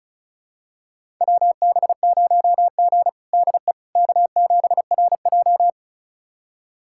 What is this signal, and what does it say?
Morse code, heard as a keyed tone, about 36 wpm, 700 Hz. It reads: WB0G DE K7RJ
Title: USA Callsign DE Callsign - 4 Characters - 35wpm